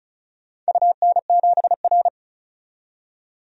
Morse code, heard as a keyed tone, about 36 wpm, 700 Hz. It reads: UN7R
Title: DXCC-100 Call Signs - 4 Characters - 35wpm